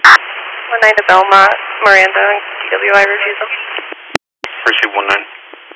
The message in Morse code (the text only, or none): none